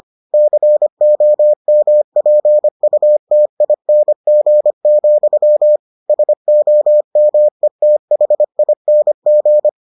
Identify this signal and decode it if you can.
Morse code, heard as a keyed tone, about 25 wpm, 600 Hz. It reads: COMPUTING, SOMETHING